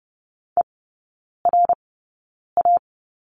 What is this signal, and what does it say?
Morse code, heard as a keyed tone, about 30 wpm, spaced as for 9 wpm, 700 Hz. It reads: ERA
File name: Lesson 15e - Number 5 - Three Characters 30wpm